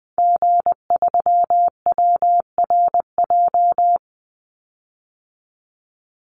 Morse code, heard as a keyed tone, about 20 wpm, 700 Hz. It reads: G3WRJ
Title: Call Signs Repeated 3x - 20wpm